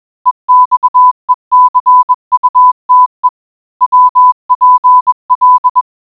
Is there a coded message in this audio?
Yes: Morse code from 0.3 to 5.8 s, heard as a keyed tone, about 21 wpm, 1000 Hz. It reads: EXECUTE WPL